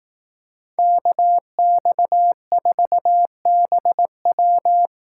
Morse code, heard as a keyed tone, about 18 wpm, 700 Hz. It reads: KX4BW
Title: Speed Racing - USA Call Signs 15wpm